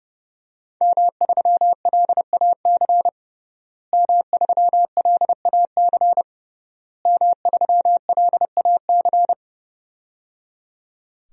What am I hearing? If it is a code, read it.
Morse code, heard as a keyed tone, about 30 wpm, 700 Hz. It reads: M3LAC M3LAC M3LAC
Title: DXCC-100 Call Signs - 5 Characters - Repeated 3x - 30wpm